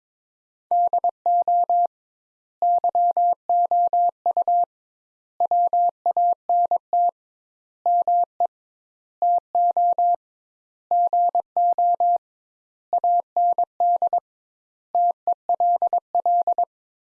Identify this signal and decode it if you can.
Morse code, heard as a keyed tone, about 22 wpm, 700 Hz. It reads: DO YOU WANT ME TO GO AND TELL
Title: Sentences Top 200 Words - 22wpm